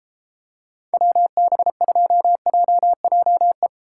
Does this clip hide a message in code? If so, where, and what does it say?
Yes: Morse code from 0.9 to 3.7 s, heard as a keyed tone, about 31 wpm, 700 Hz. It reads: WB2JJE